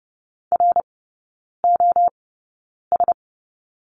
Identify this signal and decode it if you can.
Morse code, heard as a keyed tone, about 30 wpm, spaced as for 9 wpm, 700 Hz. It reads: ROS
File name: course / 30:10wpm / 087 Lesson 15e - Number 5 - Three Characters 30wpm